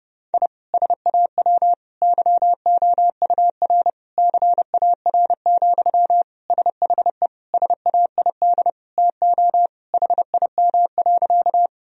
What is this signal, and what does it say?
Morse code, heard as a keyed tone, about 30 wpm, 700 Hz. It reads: I SAW YOUR CAR, SHE SAID TO HIM.
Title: Sentences from Top 1000 Words - Review Part 1 - 30wpm